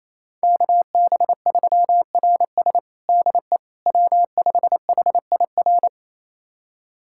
Morse code, heard as a keyed tone, about 28 wpm, 700 Hz. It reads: KB3RS DE W5HIR
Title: USA Callsign DE Callsign 28wpm